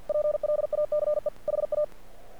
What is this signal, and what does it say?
Morse code, heard as a keyed tone, about 48 wpm, 600 Hz. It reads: PLACE HA
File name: archived / CRSnet